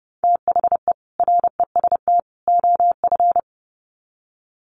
Morse code, heard as a keyed tone, about 30 wpm, 700 Hz. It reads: THE REST OF
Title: Sets of 3 Words 30wpm